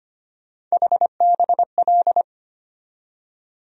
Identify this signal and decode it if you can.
Morse code, heard as a keyed tone, about 25 wpm, 700 Hz. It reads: HBL
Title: USA Call Sign Postfixes - 3 Letters 25wpm